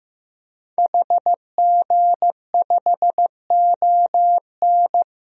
Morse code, heard as a keyed tone, about 15 wpm, 700 Hz. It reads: HG5ON
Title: DXCC-100 Call Signs - 5 Characters - 15wpm